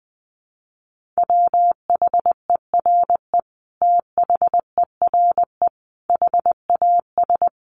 Morse code, heard as a keyed tone, about 20 wpm, 700 Hz. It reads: WHERE THERE HAS
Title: Sentences from Top 1400 Words 20wpm